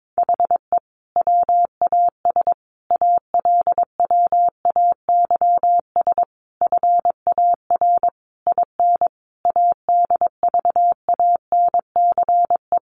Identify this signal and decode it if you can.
Morse code, heard as a keyed tone, about 22 wpm, 700 Hz. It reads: HE WAS ALWAYS FAR IN ADVANCE
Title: Sentences from Top 1400 Words 22wpm